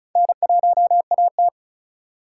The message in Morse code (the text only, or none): N1AT